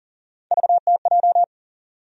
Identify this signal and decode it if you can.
Morse code, heard as a keyed tone, about 40 wpm, 700 Hz. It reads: VTJ
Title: USA Call Sign Postfixes - 3 Letters 40wpm